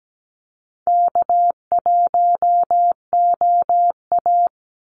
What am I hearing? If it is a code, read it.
Morse code, heard as a keyed tone, about 17 wpm, 700 Hz. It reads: K1OA
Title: Easy Call Signs - 17wpm